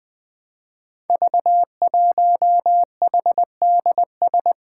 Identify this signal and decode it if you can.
Morse code, heard as a keyed tone, about 20 wpm, 700 Hz. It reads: V1HDS
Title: Five Challenging Letter-Number Combinations 20wpm